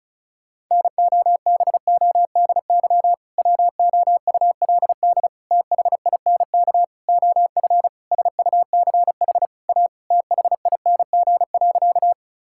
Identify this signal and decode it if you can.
Morse code, heard as a keyed tone, about 34 wpm, 700 Hz. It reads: NOBODY WOULD THINK OF SUCH A THING.
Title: Sentences from Top 1600 Words 35wpm